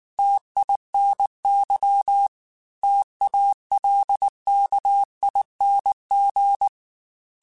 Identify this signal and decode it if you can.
Morse code, heard as a keyed tone, about 19 wpm, 800 Hz. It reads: TINY TALKING